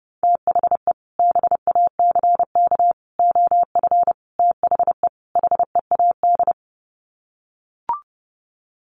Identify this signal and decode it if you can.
Morse code, heard as a keyed tone, about 30 wpm, 700 Hz. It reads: THE BACK OF THE HEAD
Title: Sets of 5 Words Encore2 30wpm